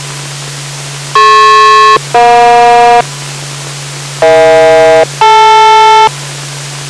Aircraft SELCAL, heard as two pairs of tones. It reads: DP-EJ EH-DL